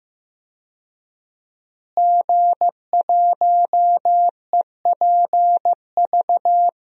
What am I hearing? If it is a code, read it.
Morse code, heard as a keyed tone, about 15 wpm, 700 Hz. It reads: G1EPV
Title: DXCC-100 Call Signs - 5 Characters - Repeated 2x - 15wpm